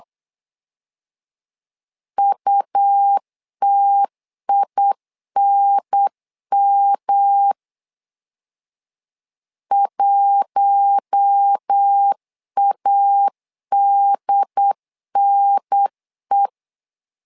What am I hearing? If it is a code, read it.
Morse code, heard as a keyed tone, about 8 wpm, 800 Hz. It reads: UTINM 1ADNE
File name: LEZIONI / 4 Q1 8wpm 3035